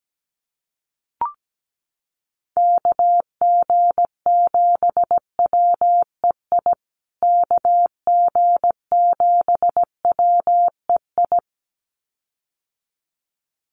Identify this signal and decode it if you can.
Morse code, heard as a keyed tone, about 17 wpm, 700 Hz. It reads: KG7WEI KG7WEI
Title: USA Call Signs Repeated 2x 17wpm